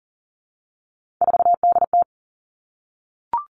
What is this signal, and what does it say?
Morse code, heard as a keyed tone, about 38 wpm, 690 Hz. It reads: 4DT
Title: Three Challenging Letter-Number Combinations 40wpm